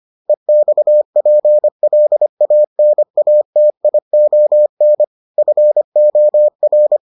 Morse code, heard as a keyed tone, about 25 wpm, 600 Hz. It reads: EXPLANATION FOR